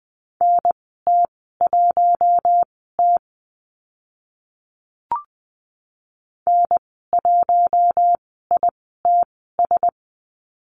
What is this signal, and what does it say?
Morse code, heard as a keyed tone, about 20 wpm, spaced as for 15 wpm, 700 Hz. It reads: NT1T N1ITS
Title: 048 Lesson 9g - Number 4 - Call Signs 20wpm